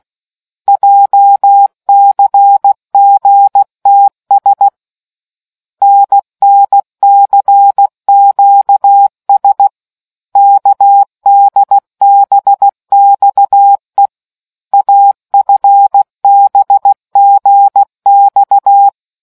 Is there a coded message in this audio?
Yes: Morse code from 0.7 to 18.9 s, heard as a keyed tone, about 16 wpm, 800 Hz. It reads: JCGTS NNCQS KDBXE AFBGX